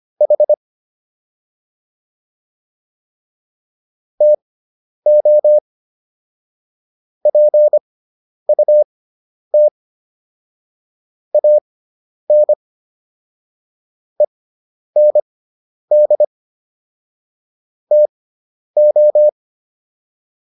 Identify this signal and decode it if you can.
Morse code, heard as a keyed tone, about 25 wpm, spaced as for 10 wpm, 600 Hz. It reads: H TO PUT AN END TO